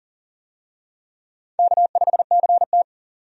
Morse code, heard as a keyed tone, about 40 wpm, 700 Hz. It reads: K5CT
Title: USA Call Signs - 4 Characters - 40wpm